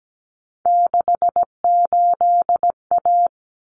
Morse code, heard as a keyed tone, about 17 wpm, 700 Hz. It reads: 68A
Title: Common Cut Numbers - 3-Digit 17wpm